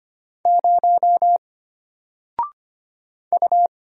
Morse code, heard as a keyed tone, about 25 wpm, 700 Hz. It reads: 0 U
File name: Single-Letter-Number-25wpm